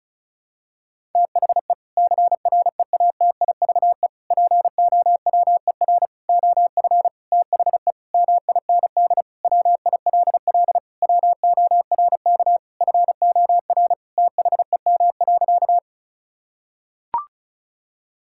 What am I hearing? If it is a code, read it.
Morse code, heard as a keyed tone, about 36 wpm, 700 Hz. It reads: THE CREATIVE POWER OF THE MIND WILL WORK FOR THEM.
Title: Sentences from Top 1000 Words - Review Part 5 - 35wpm